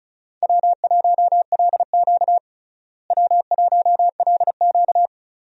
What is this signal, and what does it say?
Morse code, heard as a keyed tone, about 36 wpm, 700 Hz. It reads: W1LQ W1LQ
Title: USA Call Signs - 4 Characters - Repeated 3x - 35wpm